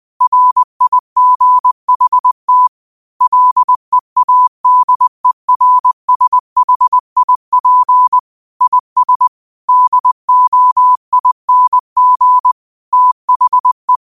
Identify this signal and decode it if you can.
Morse code, heard as a keyed tone, about 20 wpm, 1000 Hz. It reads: RIGHT LEADERSHIP IS DOING THE
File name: qotdfiles / 20wpm